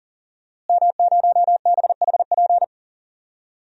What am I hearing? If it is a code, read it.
Morse code, heard as a keyed tone, about 40 wpm, 700 Hz. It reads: M0BHP
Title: DXCC-100 Call Signs - 5 Characters - Repeated 3x - 40wpm